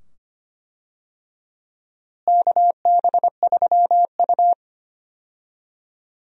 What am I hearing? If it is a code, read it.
Morse code, heard as a keyed tone, about 25 wpm, 700 Hz. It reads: KB3U
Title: USA Call Signs - 4 Characters - Repeated 2x - 25wpm